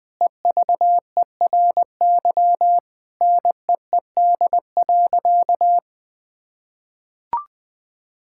Morse code, heard as a keyed tone, about 21 wpm, 700 Hz. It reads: EVERY NEED.
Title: Sentences from Top 1000 Words - Review Part 5 - 20wpm